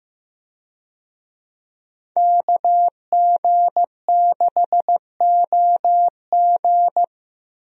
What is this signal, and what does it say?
Morse code, heard as a keyed tone, about 16 wpm, 700 Hz. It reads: KG6OG
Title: USA Call Signs - 5 Characters - Repeated 2x - 15wpm